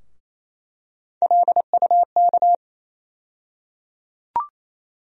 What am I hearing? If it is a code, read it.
Morse code, heard as a keyed tone, about 27 wpm, 700 Hz. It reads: LUK
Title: USA Call Sign Postfixes - 3 Letters 28wpm